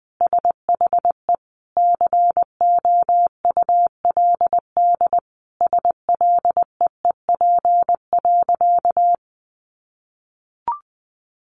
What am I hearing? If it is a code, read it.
Morse code, heard as a keyed tone, about 20 wpm, 700 Hz. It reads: SHE COULD SLEEP.
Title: Sentences from Top 500 Words 20wpm